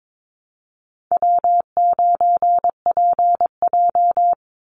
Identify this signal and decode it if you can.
Morse code, heard as a keyed tone, about 22 wpm, 690 Hz. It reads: W9PJ
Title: Easy Call Signs Repeated 2x - 22wpm